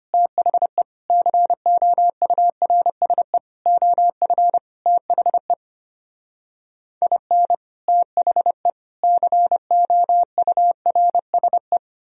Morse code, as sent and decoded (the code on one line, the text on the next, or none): - .... / -.-. --- ..- .-. ... / --- ..-. / - .... / .. -. / - .... / -.-. --- ..- .-. ... .
THE COURSE OF THE IN THE COURSE